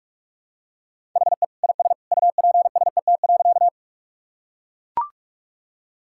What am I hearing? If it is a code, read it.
Morse code, heard as a keyed tone, about 44 wpm, 700 Hz. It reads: HE IS UPSET.
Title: Sentences from Top 1000 Words - Review Part 5 - 45wpm